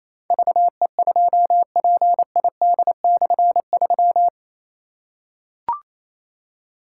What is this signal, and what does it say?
Morse code, heard as a keyed tone, about 28 wpm, 700 Hz. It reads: VE2PID/3
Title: K1USN SST QSO Practice - 28wpm